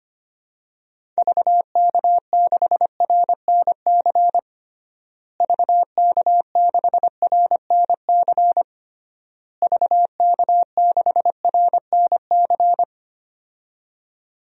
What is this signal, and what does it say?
Morse code, heard as a keyed tone, about 25 wpm, 700 Hz. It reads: VK6RNC VK6RNC VK6RNC
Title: DXCC-100 Call Signs - 6 Characters - Repeated 3x - 25wpm